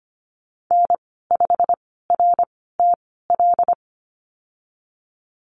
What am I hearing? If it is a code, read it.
Morse code, heard as a keyed tone, about 25 wpm, spaced as for 16 wpm, 700 Hz. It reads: N5RTL
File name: Lesson 15g - Number 5 - Call Signs 25wpm